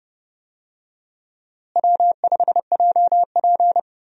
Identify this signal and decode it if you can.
Morse code, heard as a keyed tone, about 30 wpm, 700 Hz. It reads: W5JP
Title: USA Call Signs - 4 Characters - Repeated 3x - 30wpm